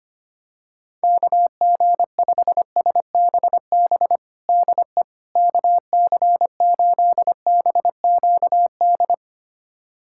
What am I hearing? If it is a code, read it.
Morse code, heard as a keyed tone, about 25 wpm, 700 Hz. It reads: KG5SBB DE KC8BQD
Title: USA Callsign DE Callsign - 6 Characters - 25wpm